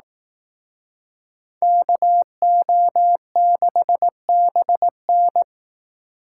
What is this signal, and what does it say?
Morse code, heard as a keyed tone, about 18 wpm, 700 Hz. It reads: KO6BN
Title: Speed Racing - USA 2x2 Call Signs 15wpm